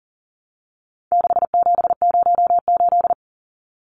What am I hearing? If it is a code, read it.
Morse code, heard as a keyed tone, about 40 wpm, 700 Hz. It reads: 6708
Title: Numbers - 4-Digit 40wpm